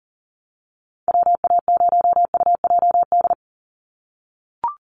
Morse code, heard as a keyed tone, about 40 wpm, 700 Hz. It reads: WA0UJD
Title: USA Call Signs 40wpm